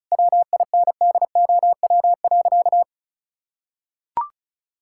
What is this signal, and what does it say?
Morse code, heard as a keyed tone, about 36 wpm, 700 Hz. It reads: WINDOW.
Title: Sentences from Top 1600 Words 35wpm